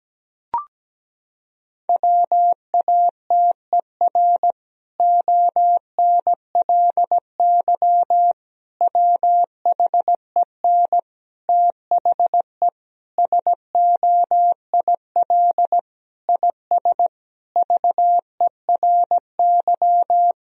Morse code, heard as a keyed tone, about 17 wpm, 700 Hz. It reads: WATER ONLY WHEN THE SOIL IS VERY
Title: Sentences from Top 1000 Words - Review Part 4 - 17wpm